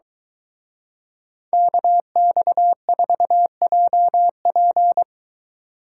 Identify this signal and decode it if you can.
Morse code, heard as a keyed tone, about 23 wpm, 700 Hz. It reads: KX4JP